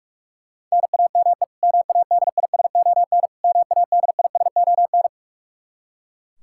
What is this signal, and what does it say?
Morse code, heard as a keyed tone, about 41 wpm, 700 Hz. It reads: NAME MADISON MADISON
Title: K1USN SST QSO Practice 45wpm